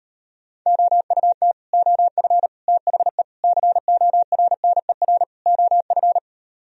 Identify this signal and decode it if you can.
Morse code, heard as a keyed tone, about 38 wpm, 700 Hz. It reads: OUT OF THE CORNER OF